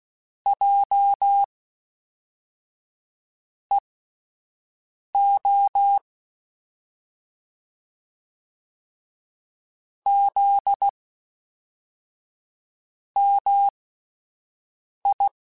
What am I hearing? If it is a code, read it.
Morse code, heard as a keyed tone, about 16 wpm, spaced as for 4 wpm, 800 Hz. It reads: JEO ZMI
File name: CW Comparisons / CW Comparisons-mp3s / CW Comparisons Broken Characters @10